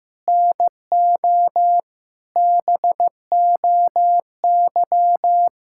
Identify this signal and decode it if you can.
Morse code, heard as a keyed tone, about 16 wpm, 700 Hz. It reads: NO BOY
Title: Sentences from Top 1000 Words - Review Part 6 - 15wpm